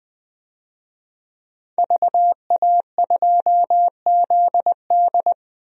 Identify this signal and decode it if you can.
Morse code, heard as a keyed tone, about 20 wpm, 700 Hz. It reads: VA2ZD
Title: DXCC-100 Call Signs - 5 Characters - Repeated 3x - 20wpm